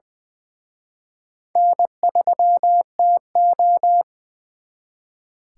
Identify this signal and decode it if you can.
Morse code, heard as a keyed tone, about 24 wpm, 700 Hz. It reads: N3TO
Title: Speed Racing - USA Call Signs 20wpm